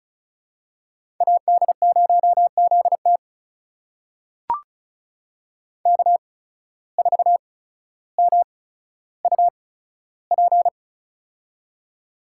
Morse code, heard as a keyed tone, about 36 wpm, spaced as for 15 wpm, 700 Hz. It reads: AD0ZT K4MUP
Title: USA Call Signs - 8x Character Spacing - 35wpm